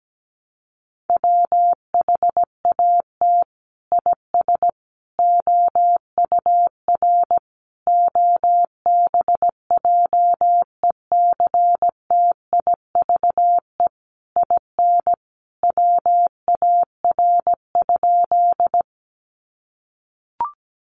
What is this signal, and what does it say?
Morse code, heard as a keyed tone, about 17 wpm, 700 Hz. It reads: WHAT IS OUR OBJECTIVE IN WAR?
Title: Sentences from Top 1200 Words 17wpm